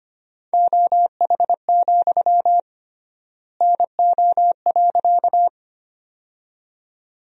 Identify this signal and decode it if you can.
Morse code, heard as a keyed tone, about 25 wpm, 700 Hz. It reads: OH, NO.
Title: 3x Word-Spacing - Sentences from Top 500 Words - 25wpm